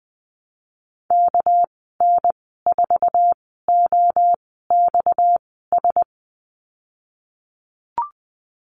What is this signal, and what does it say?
Morse code, heard as a keyed tone, about 19 wpm, spaced as for 14 wpm, 700 Hz. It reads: KN4OXS
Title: 231 Lesson 37g - Letter x - Call Signs 20wpm